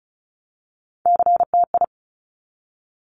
Morse code, heard as a keyed tone, about 36 wpm, 700 Hz. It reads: CTI
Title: Top Three Letter Combinations in Words 35wpm